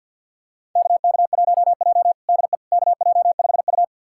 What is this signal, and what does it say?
Morse code, heard as a keyed tone, about 52 wpm, 700 Hz. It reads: KK1J DE KJ5U
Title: USA Callsign DE Callsign - 4 Characters - 50wpm